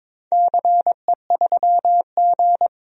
Morse code, heard as a keyed tone, about 22 wpm, 700 Hz. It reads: CE3G
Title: DXCC-100 Call Signs - 4 Characters - 22wpm